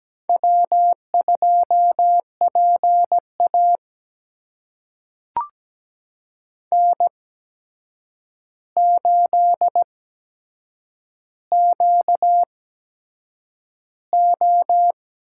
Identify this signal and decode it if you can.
Morse code, heard as a keyed tone, about 17 wpm, spaced as for 7 wpm, 700 Hz. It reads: W2PA N8QO
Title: USA Call Signs - 4 Characters - 8x Character Spacing - 17wpm